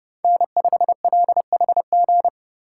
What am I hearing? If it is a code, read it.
Morse code, heard as a keyed tone, about 32 wpm, 700 Hz. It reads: N5LHG
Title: USA Call Signs - 5 Characters - 3x Character Spacing - 30wpm